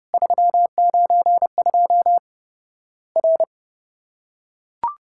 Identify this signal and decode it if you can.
Morse code, heard as a keyed tone, about 30 wpm, 700 Hz. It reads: 392 R
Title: Worked All Europe DX Contest - QTC Exchange - 30wpm